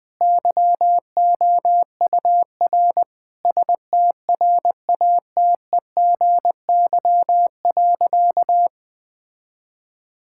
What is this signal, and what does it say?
Morse code, heard as a keyed tone, about 20 wpm, 700 Hz. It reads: YOUR STRATEGY.
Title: Sentences from Top 1000 Words - Review Part 5 - 20wpm